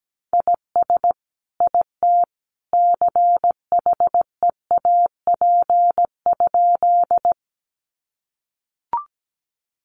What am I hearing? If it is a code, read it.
Morse code, heard as a keyed tone, about 17 wpm, 700 Hz. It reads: IS IT CHEAP?
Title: Sentences from Top 1200 Words 17wpm